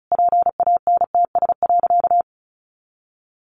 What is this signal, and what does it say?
Morse code, heard as a keyed tone, about 34 wpm, 700 Hz. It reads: PANTS.
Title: A Sound Of Thunder 35wpm